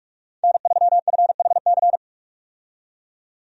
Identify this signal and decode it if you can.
Morse code, heard as a keyed tone, about 46 wpm, 700 Hz. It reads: N3FHC
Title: USA Call Signs - 5 Characters - 45wpm